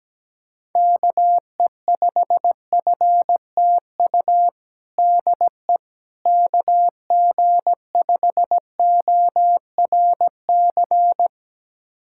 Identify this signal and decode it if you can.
Morse code, heard as a keyed tone, about 17 wpm, 700 Hz. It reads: KE5FTU DE KG5ORC
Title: USA Callsign DE Callsign 17wpm